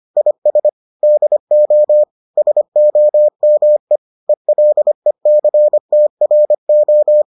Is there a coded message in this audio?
Yes: Morse code from 0.2 to 7.2 s, heard as a keyed tone, about 25 wpm, 600 Hz. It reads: IS DO SOME ELECTRO